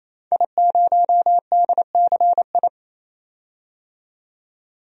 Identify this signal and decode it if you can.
Morse code, heard as a keyed tone, about 28 wpm, 700 Hz. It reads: I0DCI
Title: DXCC-100 Call Signs - 5 Characters - 28wpm